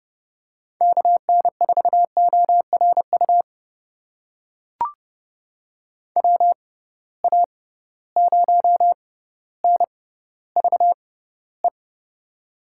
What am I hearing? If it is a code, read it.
Morse code, heard as a keyed tone, about 32 wpm, spaced as for 16 wpm, 700 Hz. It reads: KN4ORU WA0NVE
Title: USA Call Signs - 6 Characters - 6x Character Spacing - 30wpm